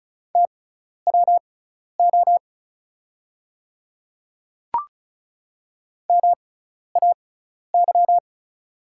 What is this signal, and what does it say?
Morse code, heard as a keyed tone, about 36 wpm, spaced as for 12 wpm, 700 Hz. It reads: TWO MAY
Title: 6x Character Spacing - Top 300 Words - 35wpm